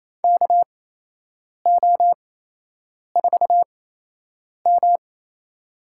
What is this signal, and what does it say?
Morse code, heard as a keyed tone, about 27 wpm, spaced as for 8 wpm, 700 Hz. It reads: KO4M
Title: USA Call Signs - 8x Character Spacing - 28wpm